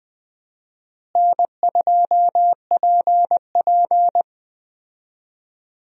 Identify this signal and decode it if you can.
Morse code, heard as a keyed tone, about 20 wpm, 700 Hz. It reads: N2PP
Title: USA Call Signs - 4 Characters - Repeated 2x - 20wpm